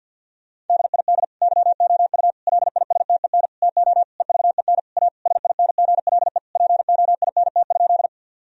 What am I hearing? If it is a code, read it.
Morse code, heard as a keyed tone, about 50 wpm, 700 Hz. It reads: DID YOU LISTEN TO EVEN A SINGLE POINT?
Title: Sentences from Top 1000 Words - Review Part 4 - 50wpm